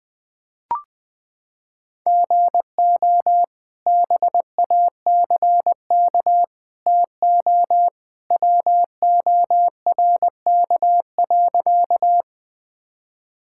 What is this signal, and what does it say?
Morse code, heard as a keyed tone, about 20 wpm, 700 Hz. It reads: GO BACK TO WORK.
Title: Sentences with 2-Word Phrases - 20wpm